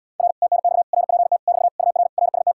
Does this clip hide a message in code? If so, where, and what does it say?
Yes: Morse code from 0.2 to 2.5 s, heard as a keyed tone, about 80 wpm, 700 Hz. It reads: R TNX RPT <BT> UR RST